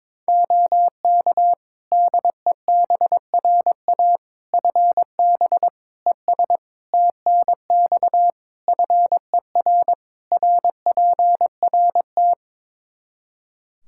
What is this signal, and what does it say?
Morse code, heard as a keyed tone, about 22 wpm, 700 Hz. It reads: OK DEBRA FB ES TNX FER RPRT
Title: Practice QSO - Exchange 2 - Info - 22wpm